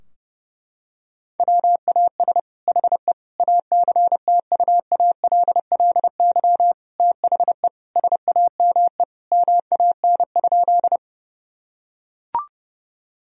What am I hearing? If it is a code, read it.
Morse code, heard as a keyed tone, about 32 wpm, 700 Hz. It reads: WAS HE ACTUALLY THE SAME MAN?
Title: Sentences from Top 1000 Words - Review Part 4 - 30wpm